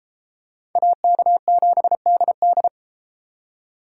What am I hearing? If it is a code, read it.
Morse code, heard as a keyed tone, about 30 wpm, 700 Hz. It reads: AK7DD